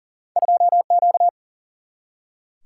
Morse code, heard as a keyed tone, about 40 wpm, 700 Hz. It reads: 2 Q
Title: Single-Letter Single-Number 40wpm